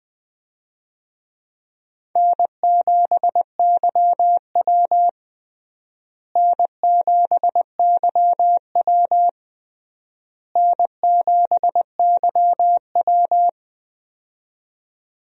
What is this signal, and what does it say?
Morse code, heard as a keyed tone, about 20 wpm, 700 Hz. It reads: N7YW N7YW N7YW
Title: USA Call Signs - 4 Characters - Repeated 3x - 20wpm